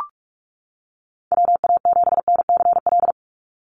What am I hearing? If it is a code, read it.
Morse code, heard as a keyed tone, about 44 wpm, 700 Hz. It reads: WA7NCL
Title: NA SOTA Activator Callsigns 45wpm